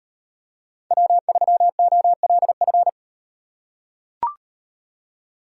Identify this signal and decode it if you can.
Morse code, heard as a keyed tone, about 38 wpm, 700 Hz. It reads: W3OLF